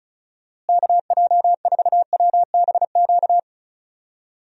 Morse code, heard as a keyed tone, about 34 wpm, 700 Hz. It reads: KJ4WBQ
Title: USA Call Signs - 6 Characters - Repeated 2x - 35wpm